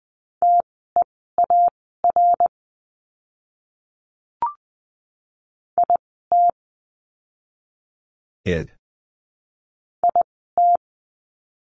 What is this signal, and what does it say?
Morse code, heard as a keyed tone, about 20 wpm, spaced as for 14 wpm, 700 Hz. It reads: TEAR IT IT
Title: 061 Lesson 11f - Letter h - Words 20wpm